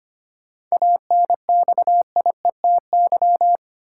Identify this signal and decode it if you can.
Morse code, heard as a keyed tone, about 25 wpm, 700 Hz. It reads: ANXIETY
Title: Sentences from Top 1600 Words - New Word Drill Alt - 25wpm